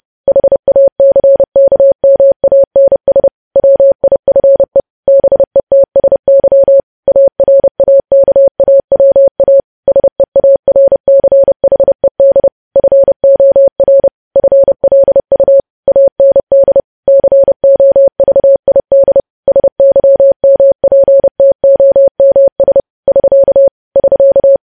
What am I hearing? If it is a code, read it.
Morse code, heard as a keyed tone, about 30 wpm, 550 Hz. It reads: HACKMANS WIFE BETSY ARAKAWA SEARCHED FOR FLU AND COVID SYMPTOMS <SK> <SK>
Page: news headlines in morse code at 30 wpm